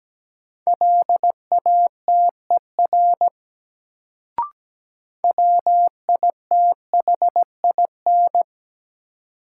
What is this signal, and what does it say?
Morse code, heard as a keyed tone, about 17 wpm, 700 Hz. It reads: LATER WITHIN
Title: Sentences from Top 700 Words - New Word Drill - 17wpm